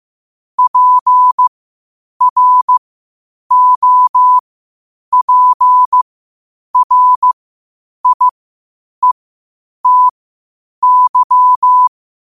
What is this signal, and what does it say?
Morse code, heard as a keyed tone, about 15 wpm, spaced as for 8 wpm, 1000 Hz. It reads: PROPRIETY
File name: qotdfiles / 8wpm